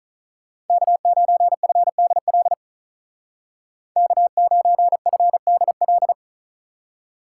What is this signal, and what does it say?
Morse code, heard as a keyed tone, about 44 wpm, 700 Hz. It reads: K9FDL K9FDL
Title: Speed Racing - USA 1x3 Call Signs 35wpm